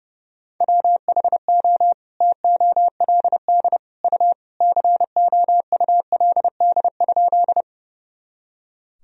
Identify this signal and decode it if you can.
Morse code, heard as a keyed tone, about 30 wpm, 700 Hz. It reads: WHO TOLD U COULD?
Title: Sentences from Top 400 Words and Abbreviations 30wpm